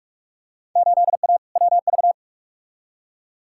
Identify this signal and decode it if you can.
Morse code, heard as a keyed tone, about 42 wpm, 700 Hz. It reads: 8A WV
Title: Field Day Exchanges Encore - 45wpm